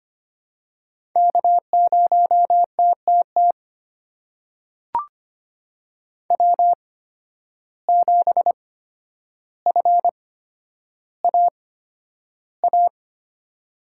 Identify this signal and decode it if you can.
Morse code, heard as a keyed tone, about 25 wpm, spaced as for 11 wpm, 700 Hz. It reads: K0TTT W7FAA
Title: USA Call Signs - 5 Characters - 8x Character Spacing - 25wpm